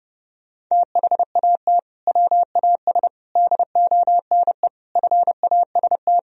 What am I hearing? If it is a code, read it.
Morse code, heard as a keyed tone, about 30 wpm, 700 Hz. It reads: THAT WAS DONE FAST
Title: Sentences from Top 1000 Words - Review Part 2 - 30wpm